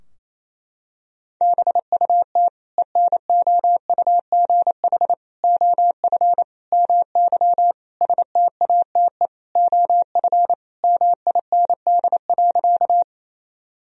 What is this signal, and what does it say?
Morse code, heard as a keyed tone, about 28 wpm, 700 Hz. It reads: BUT ENOUGH OF MY STATE OF MIND.
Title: Sentences from Top 1000 Words - Review Part 2 - 28wpm